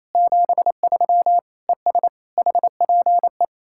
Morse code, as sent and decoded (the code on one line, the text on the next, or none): --... ...-- / . ... / .... .--. .
73 ES HPE